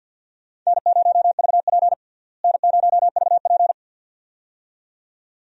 Morse code, heard as a keyed tone, about 46 wpm, 700 Hz. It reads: N0VP N0VP
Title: USA Call Signs - 4 Characters - Repeated 2x - 50wpm